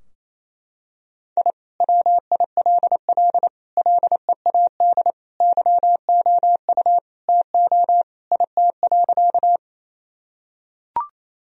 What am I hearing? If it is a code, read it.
Morse code, heard as a keyed tone, about 28 wpm, 700 Hz. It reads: I WILL LEAD YOU TO IT.
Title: Sentences from Top 1000 Words - Review Part 2 - 28wpm